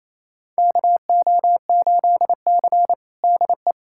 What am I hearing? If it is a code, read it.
Morse code, heard as a keyed tone, about 28 wpm, 700 Hz. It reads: KO8C DE
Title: USA Callsign DE Callsign - 4 Characters - 28wpm